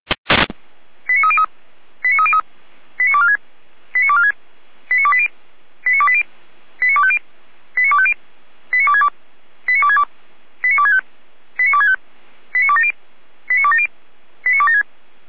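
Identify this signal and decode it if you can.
Five-tone selective call (ZVEI), heard as quick runs of five tones, tones of about 70 ms. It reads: 89303 89303 89246 89246 89280 89280 89250 89250 89262 89262 89265 89265 89270 89270 89276
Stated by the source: ZVEI five-tone selective call received with an SDR, from the Feuerwehr Orlamünde audio archive.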